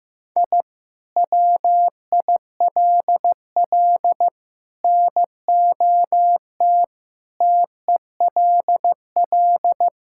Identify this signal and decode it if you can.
Morse code, heard as a keyed tone, about 15 wpm, 700 Hz. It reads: I WILL NOT TELL
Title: Sentences from Top 1000 Words - Review Part 5 - 15wpm